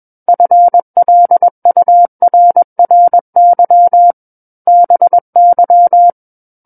Morse code, heard as a keyed tone, about 21 wpm, 700 Hz. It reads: FLURRY BY
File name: Book One / 12-Chapter 12 - 21 WPM